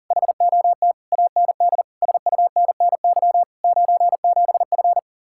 Morse code, heard as a keyed tone, about 40 wpm, 700 Hz. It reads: HOT AND SUNNY 97F